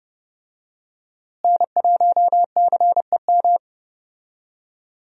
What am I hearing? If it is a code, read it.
Morse code, heard as a keyed tone, about 30 wpm, 700 Hz. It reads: N1CEM
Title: USA Call Signs - 5 Characters - Repeated 2x - 30wpm